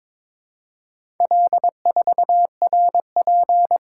Morse code, heard as a keyed tone, about 22 wpm, 700 Hz. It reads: L4RP
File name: practice / 22wpm / Four Letter-Number Combinations - 22wpm